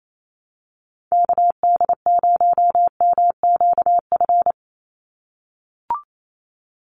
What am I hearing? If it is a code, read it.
Morse code, heard as a keyed tone, about 28 wpm, 700 Hz. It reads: KD0MQF
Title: USA Call Signs 28wpm